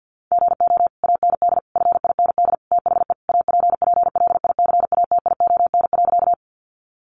Morse code, heard as a keyed tone, about 50 wpm, 700 Hz. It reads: GO AND FIND THE APPLICATION.
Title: Sentences from Top 1200 Words 50wpm